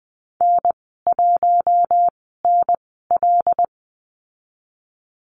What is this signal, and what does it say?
Morse code, heard as a keyed tone, about 20 wpm, spaced as for 15 wpm, 700 Hz. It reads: N1NL
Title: 122 Lesson 20g - Number 3 - Call Signs 20wpm